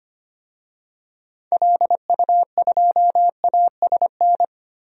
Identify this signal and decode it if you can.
Morse code, heard as a keyed tone, about 25 wpm, 700 Hz. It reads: LU2ASN
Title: DXCC-100 Call Signs - 6 Characters - 25wpm